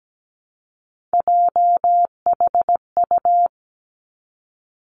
Morse code, heard as a keyed tone, about 17 wpm, 700 Hz. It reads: JHU
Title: Three Challenging Letter-Number Combinations 17wpm